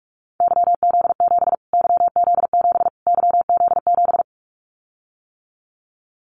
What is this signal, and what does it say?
Morse code, heard as a keyed tone, about 44 wpm, 700 Hz. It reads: YZ7 YZ7 YZ7
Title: Callsign Prefixes for Top 25 DXCC - Repeated 3x - 45wpm